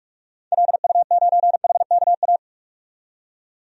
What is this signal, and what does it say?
Morse code, heard as a keyed tone, about 45 wpm, 700 Hz. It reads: LU9HKA
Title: DXCC-100 Call Signs - 6 Characters - 45wpm